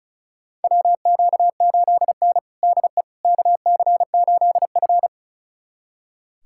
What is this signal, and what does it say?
Morse code, heard as a keyed tone, about 34 wpm, 700 Hz. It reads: WQ8N DE KC8F
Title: USA Callsign DE Callsign - 4 Characters - 35wpm